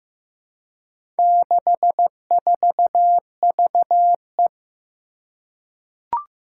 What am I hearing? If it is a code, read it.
Morse code, heard as a keyed tone, about 16 wpm, 700 Hz. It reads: 64VE
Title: Four Challenging Letter-Number Combinations 15wpm